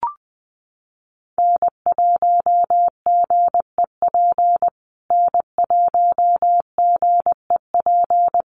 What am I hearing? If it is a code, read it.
Morse code, heard as a keyed tone, about 20 wpm, 700 Hz. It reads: N1GEP N1GEP
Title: USA Call Signs Repeated 3x 20wpm